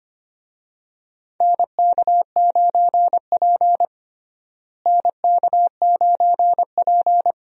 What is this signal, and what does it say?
Morse code, heard as a keyed tone, about 25 wpm, 700 Hz. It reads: NK9P NK9P
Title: USA Call Signs - 4 Characters - Repeated 2x - 25wpm